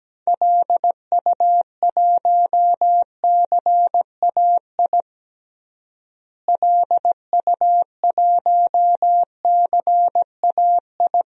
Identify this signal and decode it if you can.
Morse code, heard as a keyed tone, about 17 wpm, 700 Hz. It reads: LU1CAI LU1CAI
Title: DXCC-100 Call Signs - 6 Characters - Repeated 3x - 17wpm